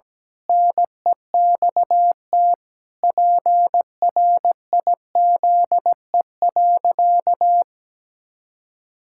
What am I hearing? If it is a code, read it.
Morse code, heard as a keyed tone, about 17 wpm, 700 Hz. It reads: NEXT PRIZE.
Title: Pangram Challenge v2 - 17wpm